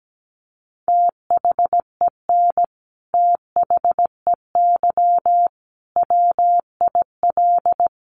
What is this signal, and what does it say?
Morse code, heard as a keyed tone, about 17 wpm, 700 Hz. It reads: THEN THEY WIL
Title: Sentences from Top 200 Words 17wpm